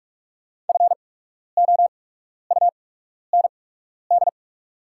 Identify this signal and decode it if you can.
Morse code, heard as a keyed tone, about 48 wpm, spaced as for 12 wpm, 690 Hz. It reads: FOUND
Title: 8x Character Spacing - Top 300 Words - 45wpm